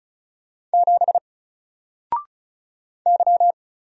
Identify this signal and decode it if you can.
Morse code, heard as a keyed tone, about 33 wpm, 700 Hz. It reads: 7 Y
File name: Single-Letter-Number-35wpm